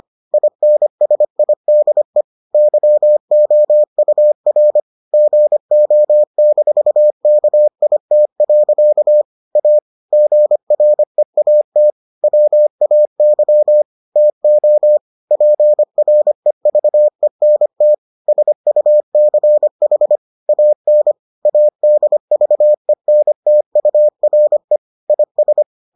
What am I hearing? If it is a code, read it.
Morse code, heard as a keyed tone, about 25 wpm, 600 Hz. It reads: INSIDE YOUR GO-KIT. A GREAT WAY TO PREVENT SUCH AN ADVENTURE IS